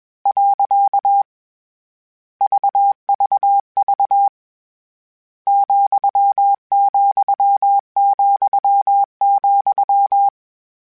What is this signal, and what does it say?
Morse code, heard as a keyed tone, about 21 wpm, 800 Hz. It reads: . VVV ,,,,
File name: Misc / Misc-mp3s / Punct @20